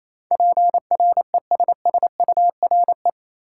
Morse code, heard as a keyed tone, about 28 wpm, 700 Hz. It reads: PRESSURE
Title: Sentences from Top 1000 Words - Review Part 6 - New Word Drill - 28wpm